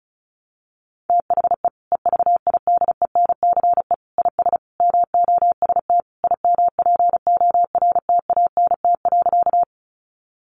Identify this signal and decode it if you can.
Morse code, heard as a keyed tone, about 34 wpm, 700 Hz. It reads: THE EVIDENCE IS MOST IMPORTANT.
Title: Sentences from Top 1300 Words 35wpm